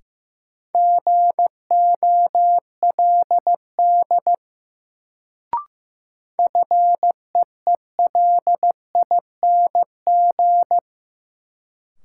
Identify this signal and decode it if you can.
Morse code, heard as a keyed tone, about 15 wpm, 700 Hz. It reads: GOLD FEELING
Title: Sentences from Top 1000 Words - Review Part 3 - New Word Drill - 15wpm